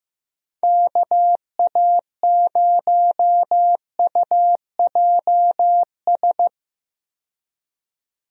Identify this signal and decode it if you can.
Morse code, heard as a keyed tone, about 15 wpm, 700 Hz. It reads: KA0UJS
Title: USA Call Signs - 6 Characters - 15wpm